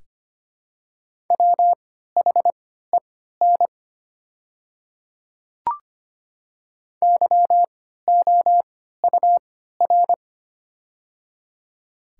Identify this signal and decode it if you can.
Morse code, heard as a keyed tone, about 25 wpm, spaced as for 14 wpm, 700 Hz. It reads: WHEN YOUR
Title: 3x Character Spacing - Top 100 Words - 25wpm